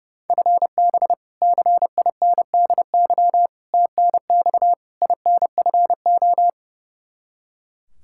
Morse code, heard as a keyed tone, about 30 wpm, 700 Hz. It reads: FB CINDY TNX INFO
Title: Practice QSO - Exchange 3 - More Info - 30wpm